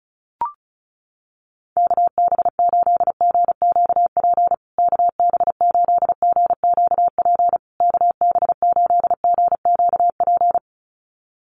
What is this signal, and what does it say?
Morse code, heard as a keyed tone, about 34 wpm, 700 Hz. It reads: KB8GQP KB8GQP KB8GQP
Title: USA Call Signs Repeated 3x 35wpm